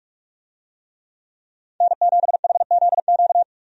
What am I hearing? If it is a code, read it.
Morse code, heard as a keyed tone, about 46 wpm, 700 Hz. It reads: N7HZQ
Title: USA Call Signs - 5 Characters - 45wpm